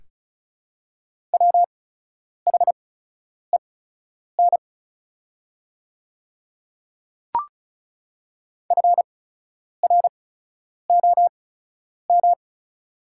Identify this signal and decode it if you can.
Morse code, heard as a keyed tone, about 36 wpm, spaced as for 10 wpm, 700 Hz. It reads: WHEN FROM
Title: 8x Character Spacing - Top 100 Words - 35wpm